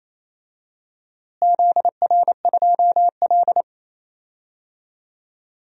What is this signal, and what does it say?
Morse code, heard as a keyed tone, about 27 wpm, 700 Hz. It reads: ZR2L
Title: DXCC-100 Call Signs - 4 Characters - Repeated 2x - 28wpm